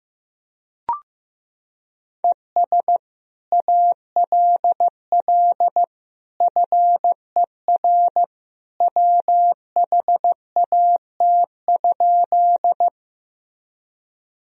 Morse code, heard as a keyed tone, about 15 wpm, 700 Hz. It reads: ES ALL FER WHAT?
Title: Sentences from Top 100 Words and Abbreviations 15wpm